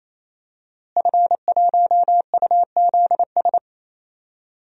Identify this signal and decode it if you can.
Morse code, heard as a keyed tone, about 28 wpm, 700 Hz. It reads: F1UZS
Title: DXCC-100 Call Signs - 5 Characters - Repeated 3x - 28wpm